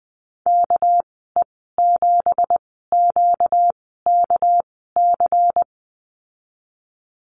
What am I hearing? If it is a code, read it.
Morse code, heard as a keyed tone, about 20 wpm, spaced as for 15 wpm, 700 Hz. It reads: KE7QKC
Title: 245 Lesson 39g - Letter z - Call Signs 20wpm